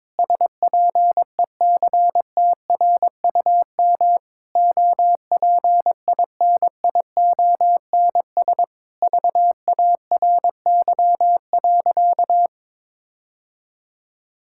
Morse code, heard as a keyed tone, about 22 wpm, 700 Hz. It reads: SPECTRUM OPINIONS VARY.
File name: Sentences with 7-Word Phrases - 22wpm